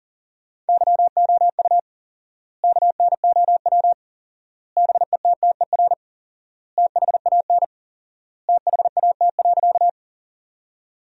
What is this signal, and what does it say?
Morse code, heard as a keyed tone, about 40 wpm, 700 Hz. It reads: YOU KNOW BETTER THAN THAT.
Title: 4x Word-Spacing - Sentences from Top 400 Words - 40wpm